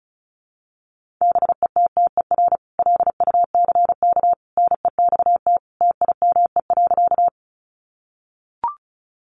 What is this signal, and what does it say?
Morse code, heard as a keyed tone, about 34 wpm, 700 Hz. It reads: BETTER LUCK NEXT TIME.
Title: Sentences from Top 1500 Words 35wpm